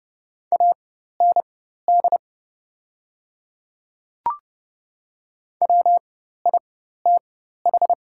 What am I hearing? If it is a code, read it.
Morse code, heard as a keyed tone, about 30 wpm, spaced as for 14 wpm, 700 Hz. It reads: AND WITH
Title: 4x Character Spacing - Top 200 Words - 30wpm